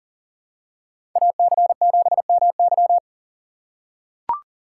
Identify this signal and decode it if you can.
Morse code, heard as a keyed tone, about 40 wpm, 700 Hz. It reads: AC7MY
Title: USA Call Signs - 6x Character Spacing - 40wpm